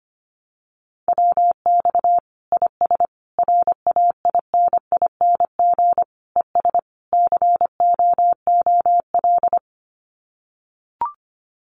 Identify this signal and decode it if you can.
Morse code, heard as a keyed tone, about 25 wpm, 700 Hz. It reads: WX IS RAINING ES COOL
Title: Common QSO Phrases 25wpm